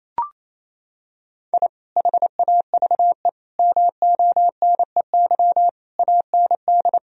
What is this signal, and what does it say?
Morse code, heard as a keyed tone, about 28 wpm, 700 Hz. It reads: I HAVE MONEY AND
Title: Sentences from Top 1000 Words - Review Part 3 - 28wpm